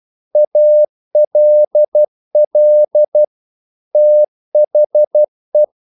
Morse code, heard as a keyed tone, about 12 wpm, 600 Hz. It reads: ALL THE
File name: CW / Practice / EnglishQuotations0312